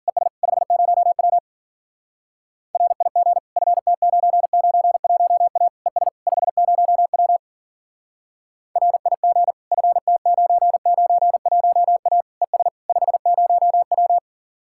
Morse code, heard as a keyed tone, about 49 wpm, 700 Hz. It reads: ES 50W RIG FT991A ES 50W RIG FT991A ES 50W
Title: Speed-Racing Common QSO Phrases Encore - 40wpm